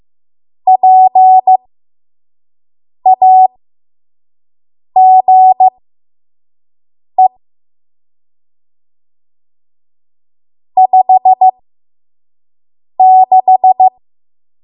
Morse code, heard as a keyed tone, about 15 wpm, spaced as for 5 wpm, 750 Hz. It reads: PAGE 56
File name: Morse / 05WPM